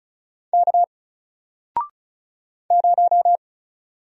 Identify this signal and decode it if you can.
Morse code, heard as a keyed tone, about 36 wpm, 700 Hz. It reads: K 0
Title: Single Character (Letter:Number:Punctuation:Prosign) Alt 35wpm